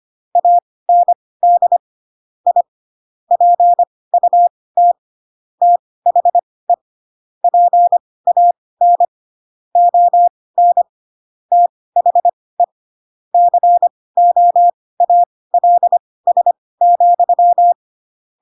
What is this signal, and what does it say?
Morse code, heard as a keyed tone, about 25 wpm, spaced as for 18 wpm, 700 Hz. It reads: AND I PUT THE PAN ON THE COALS,
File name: Pinocchio / Pinocchio ch7 12 25 ik1hge